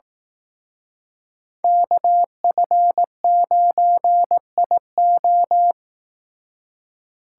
Morse code, heard as a keyed tone, about 18 wpm, 700 Hz. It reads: KF9IO
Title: Speed Racing - USA 2x2 Call Signs 15wpm